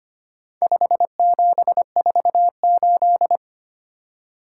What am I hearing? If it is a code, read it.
Morse code, heard as a keyed tone, about 25 wpm, 700 Hz. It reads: 5748
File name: practice / 25wpm / Four Letter-Number Combinations - 25wpm